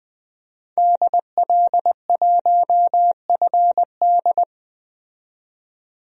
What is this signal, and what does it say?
Morse code, heard as a keyed tone, about 20 wpm, 700 Hz. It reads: DL1FD
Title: DXCC-100 Call Signs - 5 Characters - 20wpm